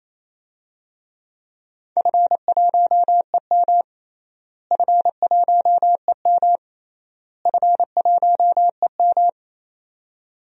Morse code, heard as a keyed tone, about 28 wpm, 700 Hz. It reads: F1EM F1EM F1EM
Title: DXCC-100 Call Signs - 4 Characters - Repeated 3x - 28wpm